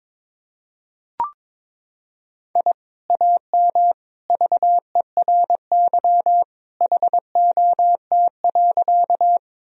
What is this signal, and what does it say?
Morse code, heard as a keyed tone, about 22 wpm, 700 Hz. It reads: I AM VERY HOT.
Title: Sentences from Top 1000 Words - Review Part 3 - 22wpm